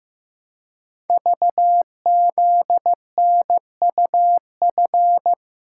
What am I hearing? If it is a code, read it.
Morse code, heard as a keyed tone, about 15 wpm, 700 Hz. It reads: VZNUF
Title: Code Groups - 5 Letter:Numbers - 15wpm